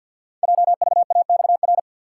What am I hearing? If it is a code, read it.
Morse code, heard as a keyed tone, about 53 wpm, 700 Hz. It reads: JVAXR
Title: Code Groups - 5 Letter:Numbers - 50wpm